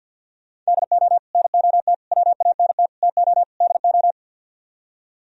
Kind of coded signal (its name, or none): Morse code